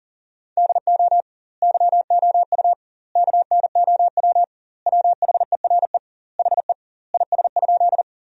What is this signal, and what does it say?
Morse code, heard as a keyed tone, about 40 wpm, 700 Hz. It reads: DO YOU KNOW WHERE HE IS?
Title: 2x Word-Spacing - Sentences from Top 300 Words - 40wpm